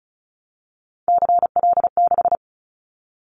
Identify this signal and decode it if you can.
Morse code, heard as a keyed tone, about 34 wpm, 700 Hz. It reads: CL6
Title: Callsign Prefixes for Top 100 DXCC - 35wpm